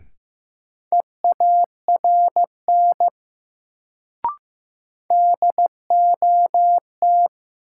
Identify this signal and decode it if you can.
Morse code, heard as a keyed tone, about 15 wpm, 700 Hz. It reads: EARN DOT
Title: Sentences from Top 1500 Words - New Word Drill - 15wpm